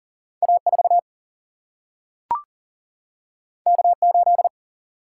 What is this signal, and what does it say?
Morse code, heard as a keyed tone, about 40 wpm, 700 Hz. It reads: A4 K8
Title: Single-Letter Single-Number 40wpm